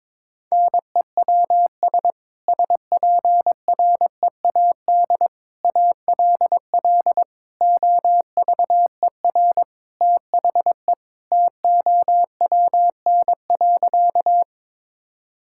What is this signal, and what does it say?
Morse code, heard as a keyed tone, about 22 wpm, 700 Hz. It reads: NEWS SPREAD ALL OVER THE TOWN.
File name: practice / 22wpm / Sentences with 3-Word Phrases - 22wpm